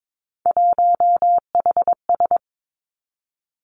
Morse code, heard as a keyed tone, about 22 wpm, 700 Hz. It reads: 1HS
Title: Three Challenging Letter-Number Combinations 22wpm